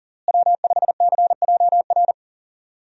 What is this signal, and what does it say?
Morse code, heard as a keyed tone, about 40 wpm, 700 Hz. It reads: W5CJR
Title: USA Call Signs - 5 Characters - Repeated 3x - 40wpm